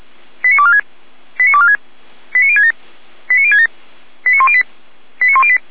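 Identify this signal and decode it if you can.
Five-tone selective call (ZVEI), heard as quick runs of five tones, tones of about 70 ms. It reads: 89246 89246 89067 89067 89108 89108
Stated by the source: ZVEI five-tone call received by SDR.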